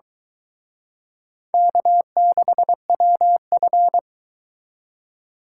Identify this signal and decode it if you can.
Morse code, heard as a keyed tone, about 23 wpm, 700 Hz. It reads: K6WF